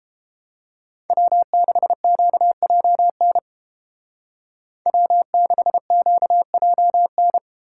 Morse code, heard as a keyed tone, about 30 wpm, 700 Hz. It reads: W6QJN W6QJN